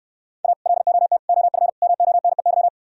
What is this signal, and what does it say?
Morse code, heard as a keyed tone, about 66 wpm, 700 Hz. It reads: A LOT OF MONEY